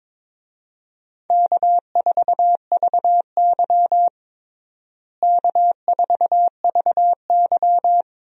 Morse code, heard as a keyed tone, about 22 wpm, 700 Hz. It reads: K4VY K4VY
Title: USA Call Signs - 4 Characters - Repeated 2x - 22wpm